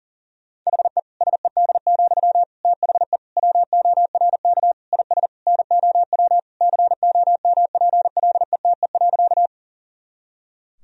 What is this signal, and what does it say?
Morse code, heard as a keyed tone, about 40 wpm, 700 Hz. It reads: HE SED, THE WORK IS NOW COMPLETE.
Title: Sentences from Top 400 Words and Abbreviations 40wpm